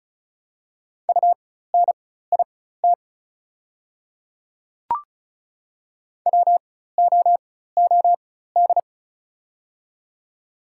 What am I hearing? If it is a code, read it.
Morse code, heard as a keyed tone, about 36 wpm, spaced as for 16 wpm, 690 Hz. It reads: UNIT WOOD
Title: 4x Character Spacing - Top 500 Words - 35wpm